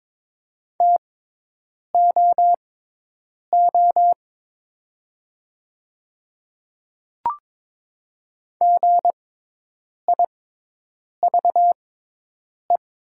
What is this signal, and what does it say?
Morse code, heard as a keyed tone, about 22 wpm, spaced as for 8 wpm, 700 Hz. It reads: TOO GIVE
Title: 6x Character Spacing - Top 200 Words - 22wpm